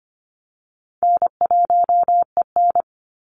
Morse code, heard as a keyed tone, about 25 wpm, 700 Hz. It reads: N1EN
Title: Easy Call Signs - 25wpm